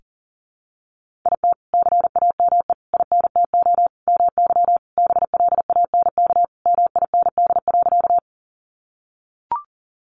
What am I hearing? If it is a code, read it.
Morse code, heard as a keyed tone, about 40 wpm, 700 Hz. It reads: IT CAME INTO MY BLANK MIND.
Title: Sentences from Top 1500 Words 40wpm